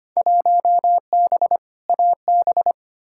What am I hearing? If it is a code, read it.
Morse code, heard as a keyed tone, about 25 wpm, 700 Hz. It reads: 1B AB
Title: Field Day Exchanges - 25wpm